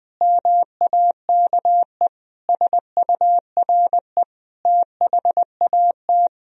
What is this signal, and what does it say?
Morse code, heard as a keyed tone, about 20 wpm, 700 Hz. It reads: MAKE SURE THAT